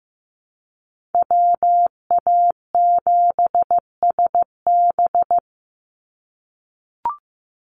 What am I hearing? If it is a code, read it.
Morse code, heard as a keyed tone, about 15 wpm, 700 Hz. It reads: WA7SB
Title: NA SOTA Chaser Callsigns 15wpm